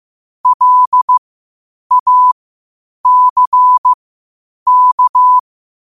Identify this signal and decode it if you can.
Morse code, heard as a keyed tone, about 15 wpm, spaced as for 8 wpm, 1000 Hz. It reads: LACK
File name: qotdfiles / 8wpm